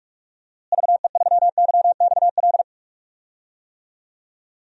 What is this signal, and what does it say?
Morse code, heard as a keyed tone, about 42 wpm, 700 Hz. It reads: VE3YXL
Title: DXCC-100 Call Signs - 6 Characters - Repeated 2x - 45wpm